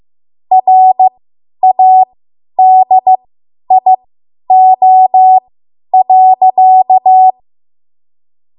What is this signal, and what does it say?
Morse code, heard as a keyed tone, about 15 wpm, spaced as for 10 wpm, 750 Hz. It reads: RADIO.
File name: Morse / 10WPM